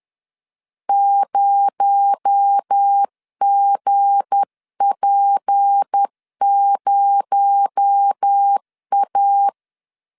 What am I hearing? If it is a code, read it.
Morse code, heard as a keyed tone, about 10 wpm, 800 Hz. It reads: 0GP0A
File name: LEZIONI / 7 GF9Z / 7 G 10wpm 2501